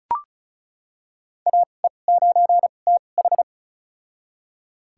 Morse code, heard as a keyed tone, about 36 wpm, spaced as for 25 wpm, 700 Hz. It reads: AE9TH
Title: USA Call Signs - 5 Characters - 2x Character Spacing - 35wpm